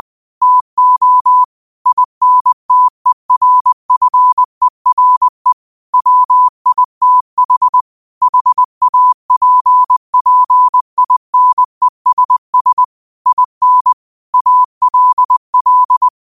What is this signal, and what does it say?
Morse code, heard as a keyed tone, about 20 wpm, 1000 Hz. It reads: TO INTERFERE WITH HAPPINESS IN ALL
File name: qotdfiles / 20wpm